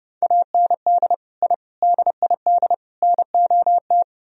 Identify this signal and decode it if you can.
Morse code, heard as a keyed tone, about 30 wpm, 700 Hz. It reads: AND I DID NOT